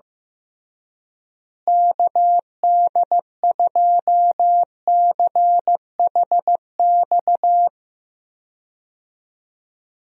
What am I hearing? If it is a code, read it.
Morse code, heard as a keyed tone, about 15 wpm, 700 Hz. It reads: KD2CHX
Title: USA Call Signs - 6 Characters - Repeated 2x - 15wpm